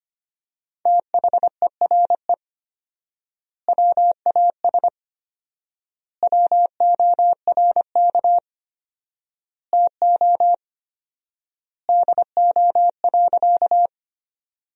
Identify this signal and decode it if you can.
Morse code, heard as a keyed tone, about 25 wpm, 700 Hz. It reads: THERE WAS WORK TO DO.
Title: 4x Word-Spacing - Sentences from Top 200 Words - 25wpm